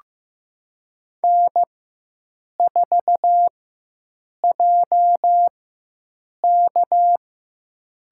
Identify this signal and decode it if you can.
Morse code, heard as a keyed tone, about 15 wpm, spaced as for 7 wpm, 700 Hz. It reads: N4JK
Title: USA Call Signs - 4x Character Spacing - 15wpm